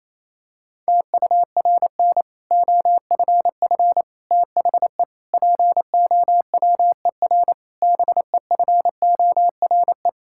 Morse code, heard as a keyed tone, about 27 wpm, 700 Hz. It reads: TURN OFF THE POWER BEFORE
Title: Sentences from Top 1000 Words - Review Part 6 - 28wpm